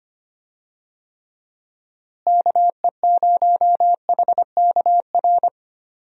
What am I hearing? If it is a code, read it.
Morse code, heard as a keyed tone, about 25 wpm, 700 Hz. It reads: KE0HKR
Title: USA Call Signs - 6 Characters - Repeated 3x - 25wpm